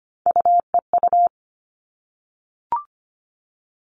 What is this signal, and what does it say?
Morse code, heard as a keyed tone, about 25 wpm, 700 Hz. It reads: UEU
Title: All Cut Numbers - 3-Digit 25wpm